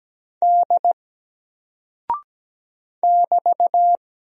Morse code, heard as a keyed tone, about 17 wpm, 700 Hz. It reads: D <BT>
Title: Single Character (Letter:Number:Punctuation:Prosign) Alt 17wpm